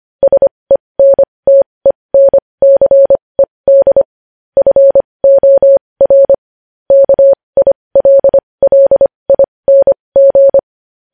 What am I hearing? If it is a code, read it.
Morse code, heard as a keyed tone, about 25 wpm, spaced as for 20 wpm, 550 Hz. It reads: SENTENCED FOR KILLING
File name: podcast / News headlines-1x-25x20